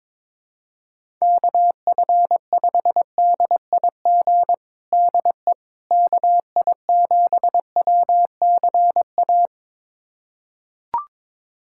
Morse code, heard as a keyed tone, about 22 wpm, 700 Hz. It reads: KF5DIG DE KI7WCA
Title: USA Callsign DE Callsign 22wpm